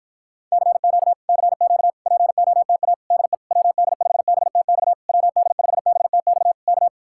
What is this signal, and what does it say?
Morse code, heard as a keyed tone, about 54 wpm, 700 Hz. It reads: CQ CQ POTA DE WD5BTX WD5BTX K